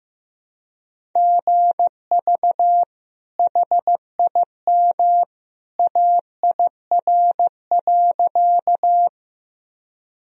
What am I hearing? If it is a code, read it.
Morse code, heard as a keyed tone, about 15 wpm, 700 Hz. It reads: GV HIM AIR.
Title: Sentences from Top 200 Words and Abbreviations 15wpm